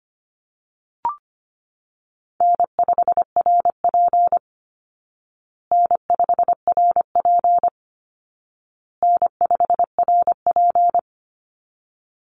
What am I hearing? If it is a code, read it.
Morse code, heard as a keyed tone, about 25 wpm, 700 Hz. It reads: N5RP N5RP N5RP
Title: Easy Call Signs Repeated 3x - 25wpm